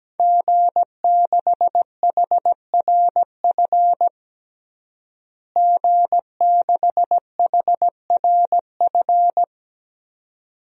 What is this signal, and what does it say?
Morse code, heard as a keyed tone, about 17 wpm, 700 Hz. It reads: G6HRF G6HRF
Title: DXCC-100 Call Signs - 5 Characters - Repeated 2x - 17wpm